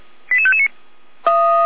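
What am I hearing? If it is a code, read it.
Five-tone selective call (ZVEI), heard as quick runs of five tones, tones of about 70 ms. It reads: 88559
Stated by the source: ZVEI five-tone call received by SDR, from the Feuerwehr Orlamünde audio archive.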